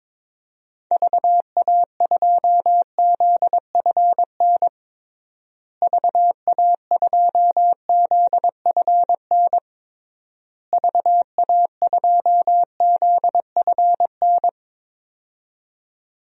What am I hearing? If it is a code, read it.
Morse code, heard as a keyed tone, about 22 wpm, 700 Hz. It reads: VA2ZFN VA2ZFN VA2ZFN
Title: DXCC-100 Call Signs - 6 Characters - Repeated 3x - 22wpm